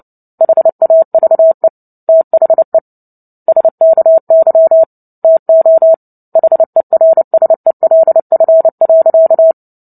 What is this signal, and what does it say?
Morse code, heard as a keyed tone, about 30 wpm, 650 Hz. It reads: HAVE THE SKY TO HERSELF.
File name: Book Two / 03-Chapter 3 - 29 WPM